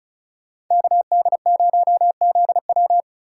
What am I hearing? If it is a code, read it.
Morse code, heard as a keyed tone, about 42 wpm, 700 Hz. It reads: KD0ZW
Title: Speed Racing - USA Call Signs 35wpm